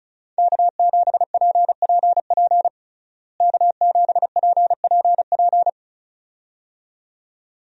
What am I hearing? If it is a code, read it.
Morse code, heard as a keyed tone, about 34 wpm, 700 Hz. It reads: K7PPP K7PPP
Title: USA Call Signs - 5 Characters - Repeated 2x - 35wpm